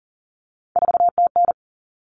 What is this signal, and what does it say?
Morse code, heard as a keyed tone, about 40 wpm, 700 Hz. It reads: 4TN
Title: All Cut Numbers - 3-Digit 40wpm